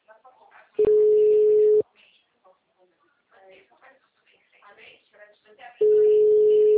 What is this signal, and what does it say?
Ringback tone (Europe, 425 Hz), with speech.